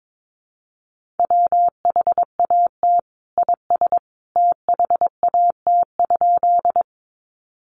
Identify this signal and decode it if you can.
Morse code, heard as a keyed tone, about 22 wpm, 700 Hz. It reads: WHAT IS THAT?
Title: Sentences from Top 100 Words 22wpm